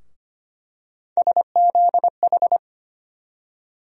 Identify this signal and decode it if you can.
Morse code, heard as a keyed tone, about 25 wpm, 700 Hz. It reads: SZH
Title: USA Call Sign Postfixes - 3 Letters 25wpm